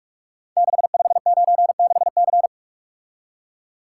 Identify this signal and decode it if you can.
Morse code, heard as a keyed tone, about 44 wpm, 700 Hz. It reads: 6596C
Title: Code Groups - 5 Letter:Numbers - 45wpm